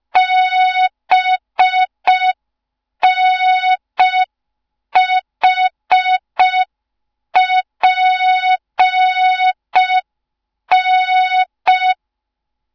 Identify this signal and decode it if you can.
Morse code, heard as a keyed tone, about 5 wpm, 3700 Hz. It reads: BNHPN
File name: Morze-HA2MN / hang / 07 GYAKORLAT